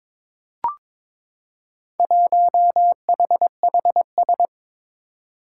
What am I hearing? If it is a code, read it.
Morse code, heard as a keyed tone, about 22 wpm, 700 Hz. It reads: 1HHS
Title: Four Challenging Letter-Number Combinations 22wpm